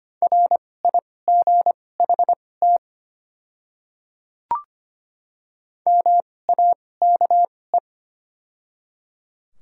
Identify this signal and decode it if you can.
Morse code, heard as a keyed tone, about 25 wpm, spaced as for 18 wpm, 700 Hz. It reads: RIGHT MAKE
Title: 2x Character Spacing - Top 200 Words - 25wpm